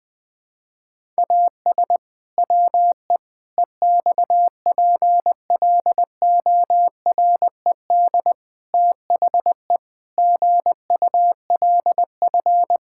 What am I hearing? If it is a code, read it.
Morse code, heard as a keyed tone, about 20 wpm, 700 Hz. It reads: AS WE EXPLORED THE GULF
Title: Pangram Challenge v2 - 20wpm